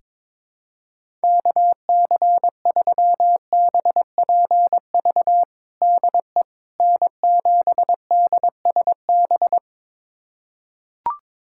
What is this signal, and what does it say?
Morse code, heard as a keyed tone, about 23 wpm, 700 Hz. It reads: KC3BPV DE N7DSB
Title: USA Callsign DE Callsign 22wpm